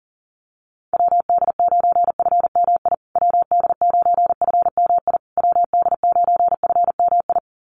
Morse code, heard as a keyed tone, about 40 wpm, 700 Hz. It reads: WD9FMI WD9FMI WD9FMI
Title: USA Call Signs Repeated 3x 40wpm